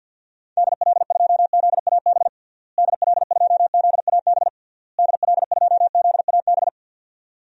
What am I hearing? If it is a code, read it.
Morse code, heard as a keyed tone, about 51 wpm, 700 Hz. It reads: DL2ZAB DL2ZAB DL2ZAB
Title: DXCC-100 Call Signs - 6 Characters - Repeated 3x - 50wpm